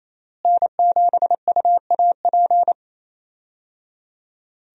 Morse code, heard as a keyed tone, about 28 wpm, 700 Hz. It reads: N7UAP
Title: USA Call Signs - 5 Characters - Repeated 3x - 28wpm